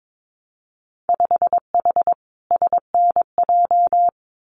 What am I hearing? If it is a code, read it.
Morse code, heard as a keyed tone, about 22 wpm, 700 Hz. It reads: 5H SNJ
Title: Winter Field Day Exchanges Encore 22wpm